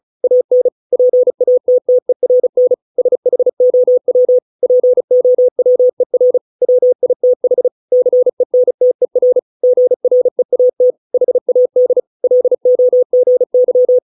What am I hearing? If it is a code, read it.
Morse code, heard as a keyed tone, about 34 wpm, 500 Hz. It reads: AN PATTERN SHOW POWER WITH CENTER GREAT HAD LOGY